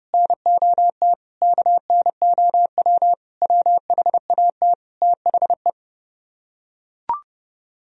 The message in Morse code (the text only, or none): NOT KNOW WHAT THE